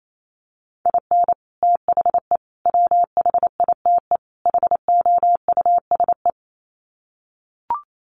Morse code, heard as a keyed tone, about 28 wpm, 700 Hz. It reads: IN THE WHITE HOUSE
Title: Sets of 4 Words 28wpm